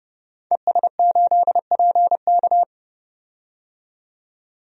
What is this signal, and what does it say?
Morse code, heard as a keyed tone, about 30 wpm, 700 Hz. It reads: ES8PK
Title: DXCC-100 Call Signs - 5 Characters - 30wpm